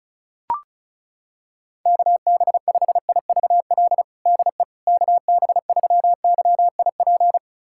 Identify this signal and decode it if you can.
Morse code, heard as a keyed tone, about 36 wpm, 700 Hz. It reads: KB5IVL DE KB3YIP
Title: USA Callsign DE Callsign - 6 Characters - 35wpm